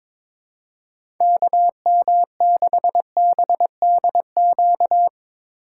Morse code, heard as a keyed tone, about 22 wpm, 700 Hz. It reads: KM6BDQ
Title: USA Call Signs - 6 Characters - 4x Character Spacing - 22wpm